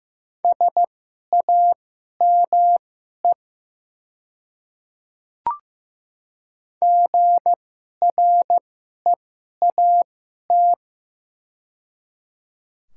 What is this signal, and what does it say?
Morse code, heard as a keyed tone, about 15 wpm, spaced as for 11 wpm, 700 Hz. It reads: SAME GREAT
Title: 2x Character Spacing - Top 300 Words - 15wpm